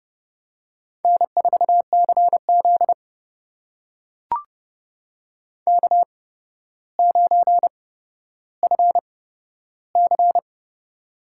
Morse code, heard as a keyed tone, about 30 wpm, spaced as for 13 wpm, 700 Hz. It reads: N4CZ K9FC
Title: USA Call Signs - 4 Characters - 8x Character Spacing - 30wpm